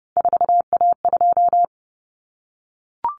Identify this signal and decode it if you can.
Morse code, heard as a keyed tone, about 30 wpm, 700 Hz. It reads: 4A2
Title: Common Cut Numbers - 3-Digit 30wpm